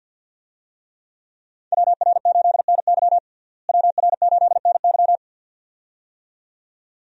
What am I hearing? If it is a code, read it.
Morse code, heard as a keyed tone, about 50 wpm, 700 Hz. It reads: WR8NY WR8NY
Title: USA Call Signs - 5 Characters - Repeated 2x - 50wpm